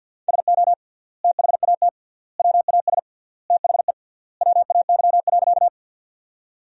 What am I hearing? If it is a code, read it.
Morse code, heard as a keyed tone, about 50 wpm, 700 Hz. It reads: SO THAT WAS THE WAY.
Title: 3x Word-Spacing - Sentences from Top 100 Words - 50wpm